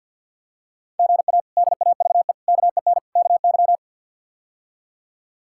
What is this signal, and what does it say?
Morse code, heard as a keyed tone, about 52 wpm, 700 Hz. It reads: GA DAVE KEN KY
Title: K1USN SST QSO Practice - 50wpm